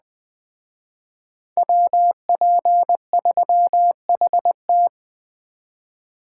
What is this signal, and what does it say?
Morse code, heard as a keyed tone, about 20 wpm, 700 Hz. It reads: WP3HT
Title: Speed Racing - USA 2x2 Call Signs 17wpm